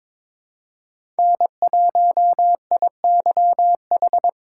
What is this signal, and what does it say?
Morse code, heard as a keyed tone, about 23 wpm, 700 Hz. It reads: N1IYH
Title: USA Call Signs - 5 Characters - 8x Character Spacing - 22wpm